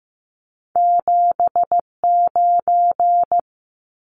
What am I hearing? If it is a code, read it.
Morse code, heard as a keyed tone, about 15 wpm, 700 Hz. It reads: 79
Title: Numbers - 2-Digit 15wpm